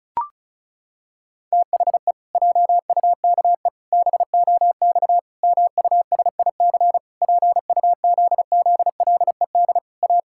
Morse code, heard as a keyed tone, about 36 wpm, 700 Hz. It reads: THE JUKE BOX MUSIC PUZZLED A
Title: Pangram Challenge v2 - 35wpm